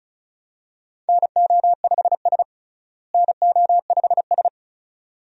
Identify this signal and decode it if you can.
Morse code, heard as a keyed tone, about 36 wpm, 700 Hz. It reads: NO5S NO5S
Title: USA Call Signs - 4 Characters - Repeated 2x - 35wpm